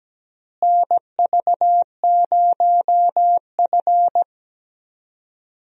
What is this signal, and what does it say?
Morse code, heard as a keyed tone, about 17 wpm, 700 Hz. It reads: NV0F
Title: USA Call Signs - 4 Characters - Repeated 3x - 17wpm